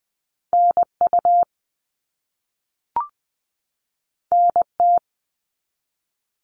Morse code, heard as a keyed tone, about 20 wpm, 700 Hz. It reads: NU NT
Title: Canadian Provinces and Territories 20wpm